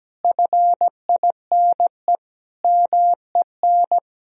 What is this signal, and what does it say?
Morse code, heard as a keyed tone, about 17 wpm, 700 Hz. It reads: FINE MEN
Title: Sentences from Top 1000 Words - Review Part 2 - 17wpm